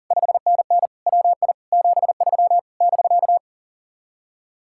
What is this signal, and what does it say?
Morse code, heard as a keyed tone, about 49 wpm, 700 Hz. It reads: 5NN WI 73 <BK>
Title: Speed-Racing Common QSO Phrases - 40wpm